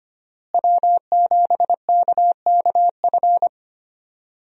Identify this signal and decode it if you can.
Morse code, heard as a keyed tone, about 25 wpm, 700 Hz. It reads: W7KKF
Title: USA Call Signs - 5 Characters - Repeated 2x - 25wpm